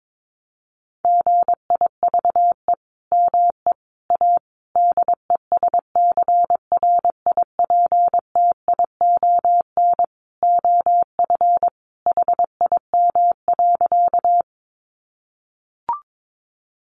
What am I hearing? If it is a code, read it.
Morse code, heard as a keyed tone, about 22 wpm, 700 Hz. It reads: GIVE ME A DESCRIPTION OF HIM.
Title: Sentences from Top 1300 Words 22wpm